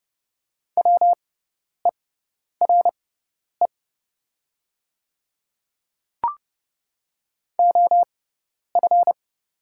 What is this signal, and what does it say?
Morse code, heard as a keyed tone, about 30 wpm, spaced as for 10 wpm, 690 Hz. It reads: WERE OF
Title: 6x Character Spacing - Top 100 Words - 30wpm